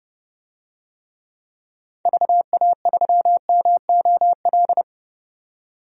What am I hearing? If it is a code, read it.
Morse code, heard as a keyed tone, about 31 wpm, 700 Hz. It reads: VA3MOL
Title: DXCC-100 Call Signs - 6 Characters - 30wpm